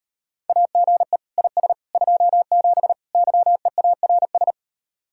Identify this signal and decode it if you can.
Morse code, heard as a keyed tone, about 38 wpm, 700 Hz. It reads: AGE IS 27 YEARS